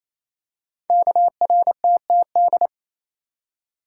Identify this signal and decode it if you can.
Morse code, heard as a keyed tone, about 28 wpm, 700 Hz. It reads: KRTTD
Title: Code Groups - 5 Letter:Numbers - 28wpm